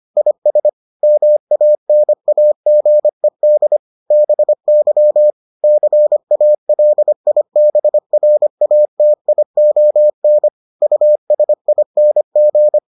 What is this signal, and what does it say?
Morse code, heard as a keyed tone, about 25 wpm, 600 Hz. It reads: IS MANAGED BY CALIBRATION USING